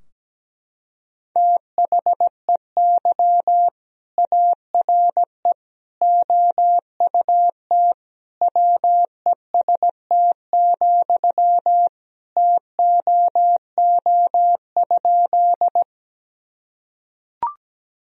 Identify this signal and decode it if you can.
Morse code, heard as a keyed tone, about 17 wpm, 700 Hz. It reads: THEY ARE OUT WEST, TOO?
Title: Sentences from Top 1000 Words - Review Part 2 - 17wpm